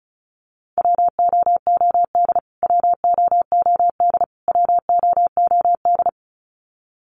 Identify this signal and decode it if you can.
Morse code, heard as a keyed tone, about 34 wpm, 700 Hz. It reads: WOOD WOOD WOOD
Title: Top 500 Words Repeated 35wpm